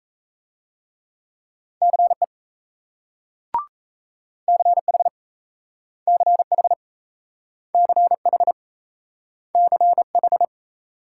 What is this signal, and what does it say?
Morse code, heard as a keyed tone, about 38 wpm, 700 Hz. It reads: CE CH CH CH CH